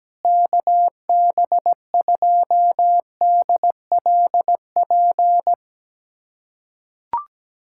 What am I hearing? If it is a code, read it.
Morse code, heard as a keyed tone, about 17 wpm, 700 Hz. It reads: KB2DLP
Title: USA Callsign DE Callsign - 6 Characters - 17wpm